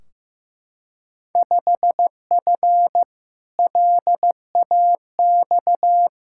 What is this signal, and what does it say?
Morse code, heard as a keyed tone, about 16 wpm, 700 Hz. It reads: 5F LAX
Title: Field Day Exchanges Encore - 15wpm